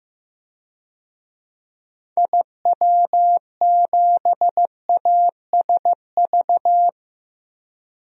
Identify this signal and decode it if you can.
Morse code, heard as a keyed tone, about 16 wpm, 700 Hz. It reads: IW7ASV
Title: DXCC-100 Call Signs - 6 Characters - Repeated 2x - 15wpm